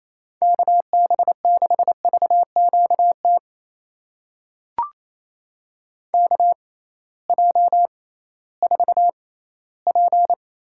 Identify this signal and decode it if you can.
Morse code, heard as a keyed tone, about 28 wpm, spaced as for 16 wpm, 700 Hz. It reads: KB6VQT KJ4P
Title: USA Call Signs - 6 Characters - 6x Character Spacing - 28wpm